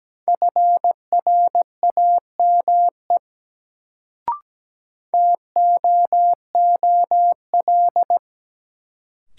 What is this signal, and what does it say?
Morse code, heard as a keyed tone, about 17 wpm, 700 Hz. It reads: FRAME TOOL
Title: Sentences from Top 1000 Words - Review Part 6 - New Word Drill - 17wpm